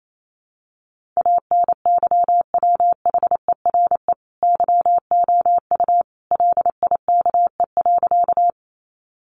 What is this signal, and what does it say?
Morse code, heard as a keyed tone, about 28 wpm, 700 Hz. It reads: ANYWHERE YOU LIKE.
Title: Sentences from Top 1400 Words 28wpm